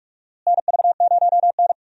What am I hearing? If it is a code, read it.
Morse code, heard as a keyed tone, about 41 wpm, 700 Hz. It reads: NV0N
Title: USA Call Signs - 4 Characters - Repeated 3x - 45wpm